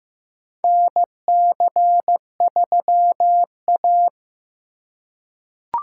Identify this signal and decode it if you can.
Morse code, heard as a keyed tone, about 15 wpm, 700 Hz. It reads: NC3A
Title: USA Call Signs - 4 Characters - 4x Character Spacing - 15wpm